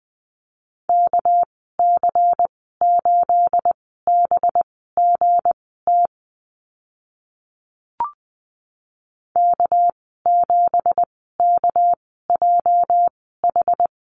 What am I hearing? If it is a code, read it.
Morse code, heard as a keyed tone, about 20 wpm, spaced as for 14 wpm, 700 Hz. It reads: KC8BGT K7KJH
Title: 219 Lesson 35g - Number 0 - Call Signs 20wpm